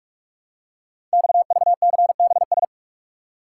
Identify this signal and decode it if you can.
Morse code, heard as a keyed tone, about 44 wpm, 700 Hz. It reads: XVCBS
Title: Code Groups - 5 Letters - 45wpm